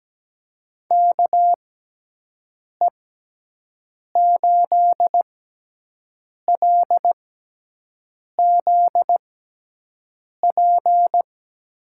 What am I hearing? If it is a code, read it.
Morse code, heard as a keyed tone, about 17 wpm, spaced as for 6 wpm, 700 Hz. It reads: KE8LZP
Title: USA Call Signs - 6 Characters - 6x Character Spacing - 17wpm